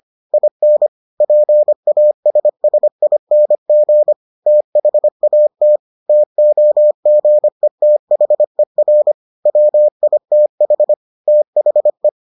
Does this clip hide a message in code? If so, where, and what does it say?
Yes: Morse code from 0.3 to 12.1 s, heard as a keyed tone, about 25 wpm, 600 Hz. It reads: IN PASSING THAT TOGETHER WITH THE